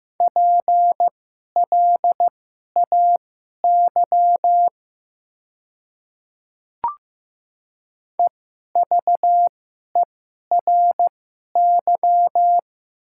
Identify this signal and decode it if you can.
Morse code, heard as a keyed tone, about 15 wpm, spaced as for 11 wpm, 700 Hz. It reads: PLAY EVERY
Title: 2x Character Spacing - Top 200 Words - 15wpm